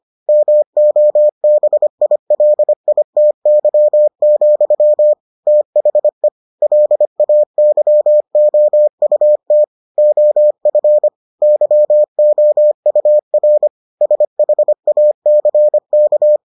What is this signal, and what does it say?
Morse code, heard as a keyed tone, about 25 wpm, 600 Hz. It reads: MOBILITY, THE LAYOUT OF YOUR SHACK